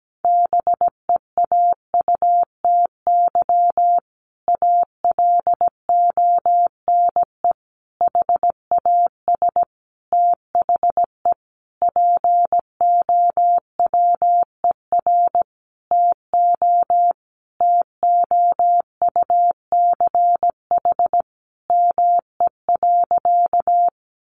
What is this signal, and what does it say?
Morse code, heard as a keyed tone, about 17 wpm, 700 Hz. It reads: BEAUTY ALONE HAS THE POWER TO TOUCH ME.
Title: Sentences from Top 1200 Words 17wpm